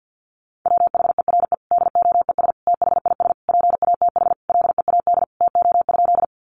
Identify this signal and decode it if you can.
Morse code, heard as a keyed tone, about 50 wpm, 700 Hz. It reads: WHERE DOES THIS PATH LEAD TO?
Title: Sentences from Top 1400 Words 50wpm